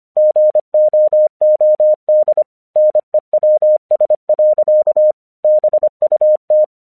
Morse code, heard as a keyed tone, about 24 wpm, 610 Hz. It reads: GOOD NEWS. BUT
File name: 1 English 25wpm / AQC28